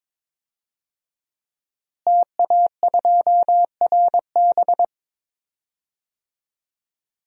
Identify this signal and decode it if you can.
Morse code, heard as a keyed tone, about 21 wpm, 700 Hz. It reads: TA2RB
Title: DXCC-100 Call Signs - 5 Characters - Repeated 2x - 22wpm